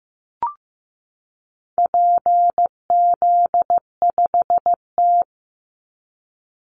Morse code, heard as a keyed tone, about 15 wpm, 700 Hz. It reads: PZ5T
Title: Call Signs Repeated 2x - 15wpm